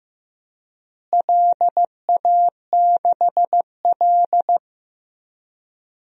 Morse code, heard as a keyed tone, about 15 wpm, 700 Hz. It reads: LA6L
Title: DXCC-100 Call Signs - 4 Characters - Repeated 3x - 15wpm